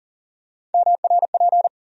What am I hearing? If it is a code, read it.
Morse code, heard as a keyed tone, about 40 wpm, 700 Hz. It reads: MRP
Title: USA Call Sign Postfixes - 3 Letters 40wpm